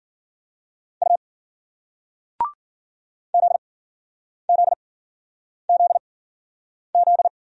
Morse code, heard as a keyed tone, about 50 wpm, spaced as for 9 wpm, 700 Hz. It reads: U ZZZZ